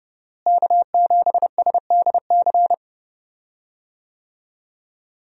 Morse code, heard as a keyed tone, about 32 wpm, 700 Hz. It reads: K7SDC
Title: USA Call Signs - 5 Characters - Repeated 3x - 30wpm